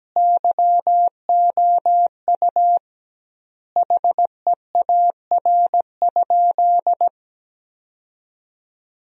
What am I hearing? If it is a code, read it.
Morse code, heard as a keyed tone, about 17 wpm, 700 Hz. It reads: YOU HEAR?
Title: 2x Word-Spacing - Sentences from Top 300 Words - 17wpm